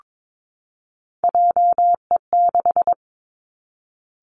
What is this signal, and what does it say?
Morse code, heard as a keyed tone, about 22 wpm, 700 Hz. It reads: JE6
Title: Three Challenging Letter-Number Combinations 22wpm